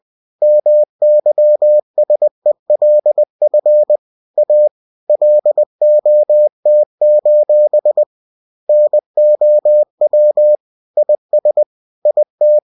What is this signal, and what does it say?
Morse code, heard as a keyed tone, about 20 wpm, 600 Hz. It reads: MYSELF A LOT: NOW IS IT
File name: cw-books / winnie-the-pooh / winnie-20wpm-page-0068